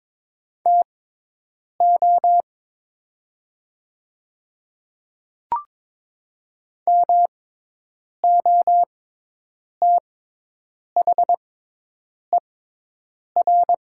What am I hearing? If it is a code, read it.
Morse code, heard as a keyed tone, about 22 wpm, spaced as for 8 wpm, 700 Hz. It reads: TO MOTHER
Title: 6x Character Spacing - Top 200 Words - 22wpm